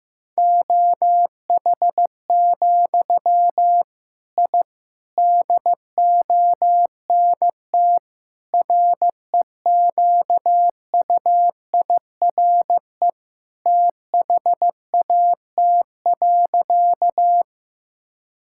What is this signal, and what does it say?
Morse code, heard as a keyed tone, about 15 wpm, 700 Hz. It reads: OH, I DONT REQUIRE THAT.
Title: Sentences from Top 1000 Words - Review Part 6 - 15wpm